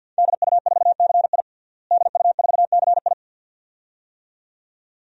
Morse code, heard as a keyed tone, about 50 wpm, 700 Hz. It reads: DU4CI DU4CI
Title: DXCC-100 Call Signs - 5 Characters - Repeated 2x - 50wpm